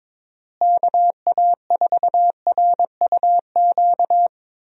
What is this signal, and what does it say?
Morse code, heard as a keyed tone, about 21 wpm, 700 Hz. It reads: KA4RUQ
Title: USA Call Signs - 6 Characters - Repeated 2x - 22wpm